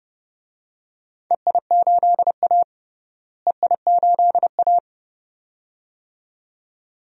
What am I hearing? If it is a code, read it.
Morse code, heard as a keyed tone, about 30 wpm, 700 Hz. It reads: EI8A EI8A
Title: DXCC-100 Call Signs - 4 Characters - Repeated 2x - 30wpm